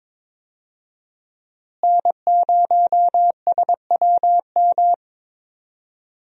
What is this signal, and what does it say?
Morse code, heard as a keyed tone, about 22 wpm, 700 Hz. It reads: N0SWM
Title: USA Call Signs - 5 Characters - Repeated 2x - 22wpm